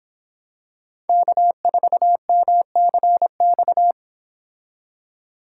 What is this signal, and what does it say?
Morse code, heard as a keyed tone, about 26 wpm, 700 Hz. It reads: K4MCX